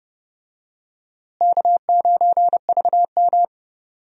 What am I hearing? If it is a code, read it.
Morse code, heard as a keyed tone, about 30 wpm, 700 Hz. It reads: K9VM
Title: USA Call Signs - 4 Characters - Repeated 3x - 30wpm